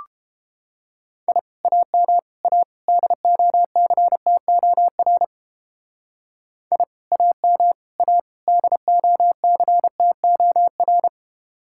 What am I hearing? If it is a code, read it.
Morse code, heard as a keyed tone, about 31 wpm, 700 Hz. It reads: I AM A DOCTOR I AM A DOCTOR